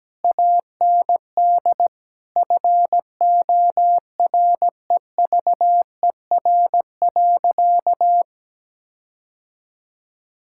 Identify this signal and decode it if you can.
Morse code, heard as a keyed tone, about 17 wpm, 700 Hz. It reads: AND FOREVER.
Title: Sentences from Top 1600 Words 17wpm